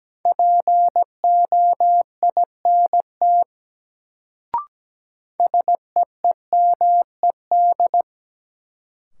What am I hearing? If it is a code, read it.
Morse code, heard as a keyed tone, about 17 wpm, 700 Hz. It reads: POINT SEEMED
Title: Sets of 3 Words - New Word Drill - 17wpm